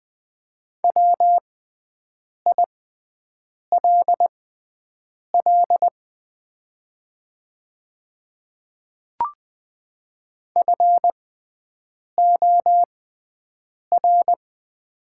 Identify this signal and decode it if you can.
Morse code, heard as a keyed tone, about 20 wpm, spaced as for 7 wpm, 690 Hz. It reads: WILL FOR
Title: 6x Character Spacing - Top 100 Words - 20wpm